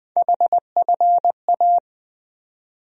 Morse code, heard as a keyed tone, about 20 wpm, 700 Hz. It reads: HFA
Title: USA Call Sign Postfixes - 3 Letters 20wpm